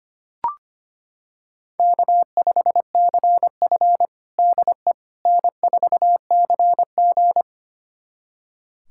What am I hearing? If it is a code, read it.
Morse code, heard as a keyed tone, about 25 wpm, 700 Hz. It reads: K5CF DE N4CG
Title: USA Callsign DE Callsign - 4 Characters - 25wpm